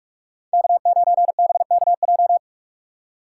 Morse code, heard as a keyed tone, about 46 wpm, 700 Hz. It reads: K9BKJ